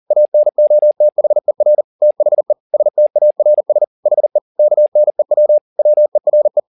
Morse code, heard as a keyed tone, about 40 wpm, 600 Hz. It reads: ANOTHER THE STARS HE KNEW WERE
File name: the old man and the sea 2 40wpm